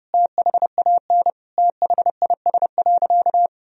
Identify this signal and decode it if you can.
Morse code, heard as a keyed tone, about 30 wpm, 700 Hz. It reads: THAN THIS.
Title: Sentences from Top 1000 Words - Review Part 3 - 30wpm